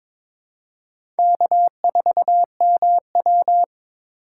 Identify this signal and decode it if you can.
Morse code, heard as a keyed tone, about 22 wpm, 700 Hz. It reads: K4MW
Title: USA Call Signs - 4 Characters - 4x Character Spacing - 22wpm